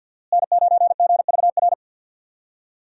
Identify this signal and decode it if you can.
Morse code, heard as a keyed tone, about 51 wpm, 700 Hz. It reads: N9GVR
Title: USA Call Signs - 5 Characters - Repeated 2x - 50wpm